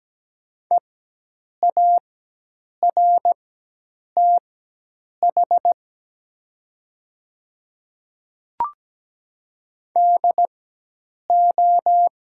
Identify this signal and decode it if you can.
Morse code, heard as a keyed tone, about 17 wpm, spaced as for 8 wpm, 700 Hz. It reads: EARTH DO
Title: 4x Character Spacing - Top 500 Words - 17wpm